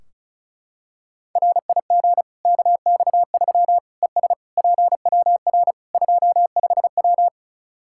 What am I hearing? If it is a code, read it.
Morse code, heard as a keyed tone, about 36 wpm, 700 Hz. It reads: RIG KX3 ES PWR 25W
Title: Practice QSO - Exchange 2 - Info - 35wpm